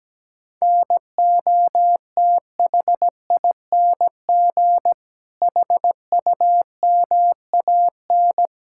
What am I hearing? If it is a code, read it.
Morse code, heard as a keyed tone, about 17 wpm, 700 Hz. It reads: NOTHING HUMAN
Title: Sentences from Top 1600 Words 17wpm